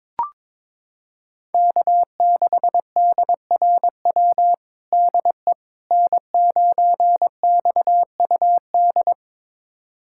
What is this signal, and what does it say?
Morse code, heard as a keyed tone, about 22 wpm, 700 Hz. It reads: K6DRW DE N9XUD
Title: USA Callsign DE Callsign - 5 Characters - 22wpm